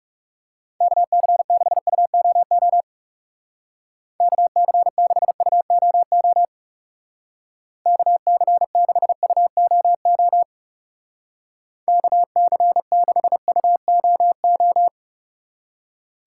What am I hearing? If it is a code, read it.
Morse code, heard as a keyed tone, about 37 wpm, 700 Hz. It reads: KC6UOO KC6UOO KC6UOO KC6UOO
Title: Speed Racing - USA 2x3 Call Signs 30wpm